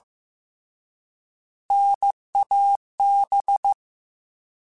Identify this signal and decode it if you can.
Morse code, heard as a keyed tone, about 15 wpm, 800 Hz. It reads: NAB
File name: Alphabetical / Alphabetical mp3s / Ns @13